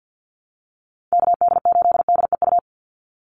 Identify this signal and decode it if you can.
Morse code, heard as a keyed tone, about 50 wpm, 700 Hz. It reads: KD8DEU
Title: NA SOTA Chaser Callsigns 50wpm